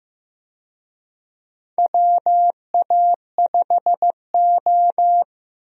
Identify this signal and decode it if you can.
Morse code, heard as a keyed tone, about 15 wpm, 700 Hz. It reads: WA5O
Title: USA Call Signs - 4 Characters - Repeated 2x - 15wpm